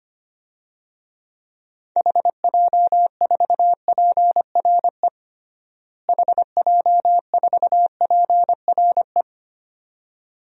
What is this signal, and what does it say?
Morse code, heard as a keyed tone, about 25 wpm, 700 Hz. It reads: HJ4PRE HJ4PRE
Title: DXCC-100 Call Signs - 6 Characters - Repeated 2x - 25wpm